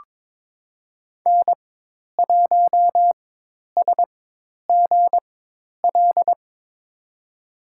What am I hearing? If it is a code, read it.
Morse code, heard as a keyed tone, about 22 wpm, spaced as for 10 wpm, 700 Hz. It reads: N1SGL
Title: USA Call Signs - 5 Characters - 4x Character Spacing - 22wpm